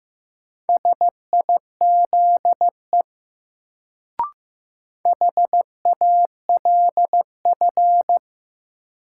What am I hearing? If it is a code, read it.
Morse code, heard as a keyed tone, about 15 wpm, 700 Hz. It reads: SIZE HALF
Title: Sentences from Top 400 Words - New Word Drill - 15wpm